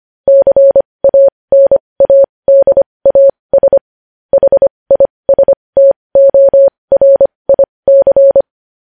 Morse code, heard as a keyed tone, about 25 wpm, spaced as for 20 wpm, 550 Hz. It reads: CANADAS HISTORIC
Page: news headlines in morse code at 20 wpm